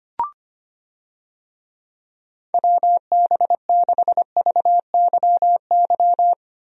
Morse code, heard as a keyed tone, about 25 wpm, 700 Hz. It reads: WB6VYY
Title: USA Call Signs - 6 Characters - Repeated 2x - 25wpm